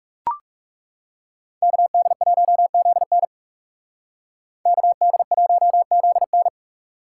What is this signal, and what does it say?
Morse code, heard as a keyed tone, about 39 wpm, 700 Hz. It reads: KD1ZN KD1ZN